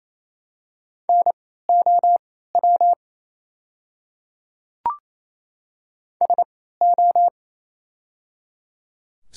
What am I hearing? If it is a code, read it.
Morse code, heard as a keyed tone, about 28 wpm, spaced as for 16 wpm, 700 Hz. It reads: NOW SO
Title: 3x Character Spacing - Top 100 Words - 28wpm